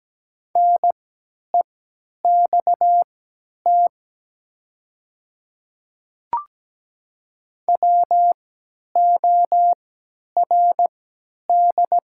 Morse code, heard as a keyed tone, about 17 wpm, spaced as for 10 wpm, 700 Hz. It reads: NEXT WORD
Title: 3x Character Spacing - Top 300 Words - 17wpm